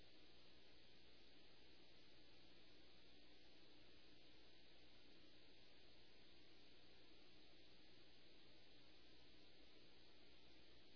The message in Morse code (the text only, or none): none